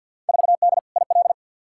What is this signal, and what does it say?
Morse code, heard as a keyed tone, about 52 wpm, 700 Hz. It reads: 4D IL
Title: Field Day Exchanges Encore - 50wpm